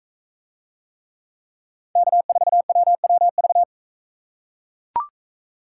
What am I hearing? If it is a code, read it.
Morse code, heard as a keyed tone, about 40 wpm, 700 Hz. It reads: K4WWV